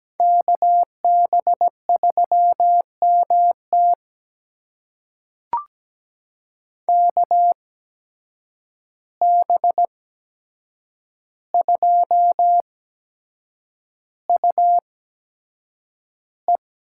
Morse code, heard as a keyed tone, about 17 wpm, spaced as for 7 wpm, 700 Hz. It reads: KB3MT KB2UE
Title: USA Call Signs - 8x Character Spacing - 17wpm